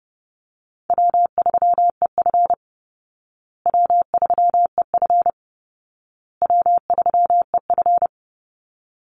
Morse code, heard as a keyed tone, about 30 wpm, 700 Hz. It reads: W3EF W3EF W3EF
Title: Easy Call Signs Repeated 3x - 30wpm